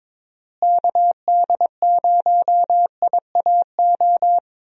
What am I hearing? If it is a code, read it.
Morse code, heard as a keyed tone, about 21 wpm, 700 Hz. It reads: KD0IAO
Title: USA Call Signs - 6 Characters - Repeated 3x - 22wpm